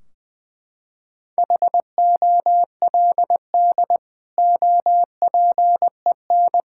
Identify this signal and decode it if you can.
Morse code, heard as a keyed tone, about 21 wpm, 700 Hz. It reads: HOLD OPEN
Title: Sentences from Top 1600 Words 20wpm